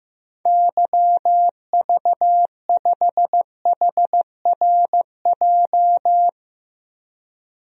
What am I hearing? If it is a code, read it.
Morse code, heard as a keyed tone, about 16 wpm, 700 Hz. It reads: YV5HRJ
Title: DXCC-100 Call Signs - 6 Characters - 15wpm